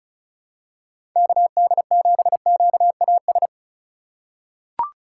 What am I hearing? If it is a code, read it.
Morse code, heard as a keyed tone, about 36 wpm, 700 Hz. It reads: KD7QAS
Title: USA Call Signs - 6 Characters - 2x Character Spacing - 35wpm